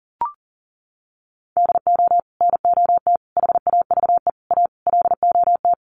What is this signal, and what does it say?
Morse code, heard as a keyed tone, about 40 wpm, 700 Hz. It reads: DO NOT HAVE A LOT
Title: Sets of 5 Words Encore 40wpm